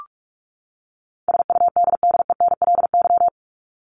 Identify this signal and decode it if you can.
Morse code, heard as a keyed tone, about 44 wpm, 700 Hz. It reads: SUDDENLY
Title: Top 1501-1750 Words 45wpm